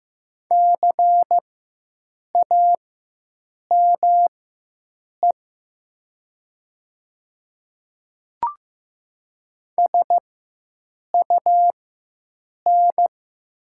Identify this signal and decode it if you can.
Morse code, heard as a keyed tone, about 15 wpm, spaced as for 7 wpm, 700 Hz. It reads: CAME SUN
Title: 4x Character Spacing - Top 300 Words - 15wpm